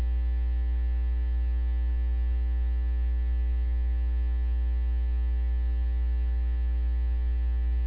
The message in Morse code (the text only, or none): none